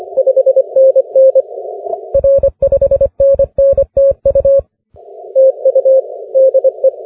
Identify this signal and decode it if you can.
Morse code, heard as a keyed tone, about 24 wpm, 530 Hz. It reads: 5NN R 5NN TU TU DE
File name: wp / pj4 w9njy 17m cw 28 mar 2011 2054z